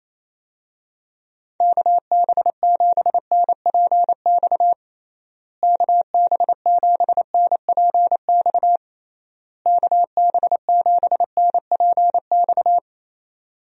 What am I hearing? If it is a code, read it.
Morse code, heard as a keyed tone, about 28 wpm, 700 Hz. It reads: KB7NPX KB7NPX KB7NPX
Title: USA Call Signs - 6 Characters - Repeated 3x - 28wpm